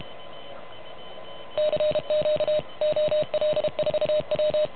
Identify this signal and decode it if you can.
Morse code, heard as a keyed tone, about 33 wpm, 610 Hz. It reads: CQ OL4W
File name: truSDX / MWC29.8